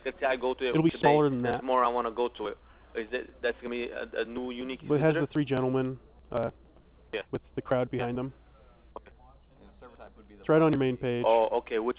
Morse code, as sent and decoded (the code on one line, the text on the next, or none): none